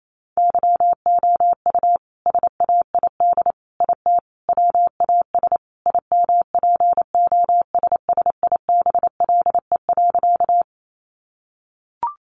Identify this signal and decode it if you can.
Morse code, heard as a keyed tone, about 28 wpm, 700 Hz. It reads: YOU SAID IT WAS IMPOSSIBLE.
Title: Sentences from Top 1100 Words 28wpm